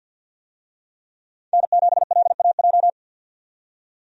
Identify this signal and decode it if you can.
Morse code, heard as a keyed tone, about 52 wpm, 700 Hz. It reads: N7LAJ
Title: USA Call Signs - 5 Characters - 50wpm